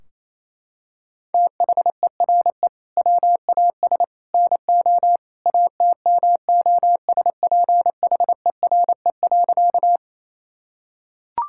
Morse code, heard as a keyed tone, about 28 wpm, 700 Hz. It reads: THERE WAS NO ATMOSPHERE.
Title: Sentences from Top 1600 Words 28wpm